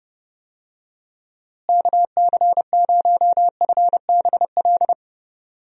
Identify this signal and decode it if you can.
Morse code, heard as a keyed tone, about 30 wpm, 700 Hz. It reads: KC0FBL
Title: USA Call Signs - 6 Characters - Repeated 2x - 30wpm